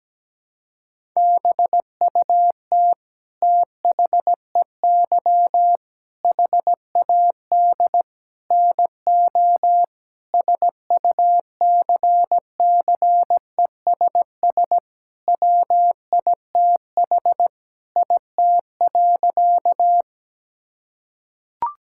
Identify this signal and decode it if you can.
Morse code, heard as a keyed tone, about 17 wpm, 700 Hz. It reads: BUT THEY HAD NO SUCCESS WITH IT.
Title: Sentences from Top 1000 Words - Review Part 6 - 17wpm